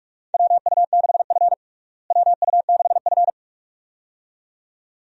Morse code, heard as a keyed tone, about 46 wpm, 700 Hz. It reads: WU6F WU6F
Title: USA Call Signs - 4 Characters - Repeated 2x - 45wpm